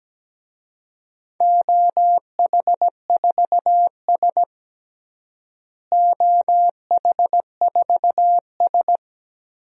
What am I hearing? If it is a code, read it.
Morse code, heard as a keyed tone, about 17 wpm, 700 Hz. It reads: OH4S OH4S
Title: DXCC-100 Call Signs - 4 Characters - Repeated 3x - 17wpm